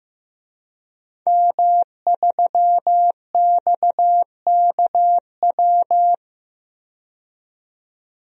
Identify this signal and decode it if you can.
Morse code, heard as a keyed tone, about 15 wpm, 700 Hz. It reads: M3XKW
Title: DXCC-100 Call Signs - 5 Characters - 15wpm